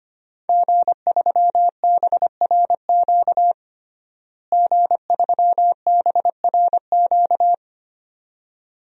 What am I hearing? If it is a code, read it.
Morse code, heard as a keyed tone, about 25 wpm, 700 Hz. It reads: G3BRQ G3BRQ
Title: DXCC-100 Call Signs - 5 Characters - Repeated 2x - 25wpm